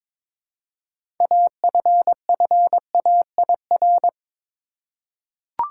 Morse code, heard as a keyed tone, about 21 wpm, 700 Hz. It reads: AFFAIR
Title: Six Letter Words - 22wpm